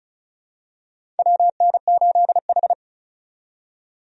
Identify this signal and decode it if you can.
Morse code, heard as a keyed tone, about 36 wpm, 700 Hz. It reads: WN8H
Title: USA Call Signs - 4 Characters - 35wpm